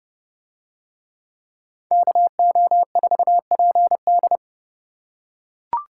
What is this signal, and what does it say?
Morse code, heard as a keyed tone, about 28 wpm, 700 Hz. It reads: KO4PD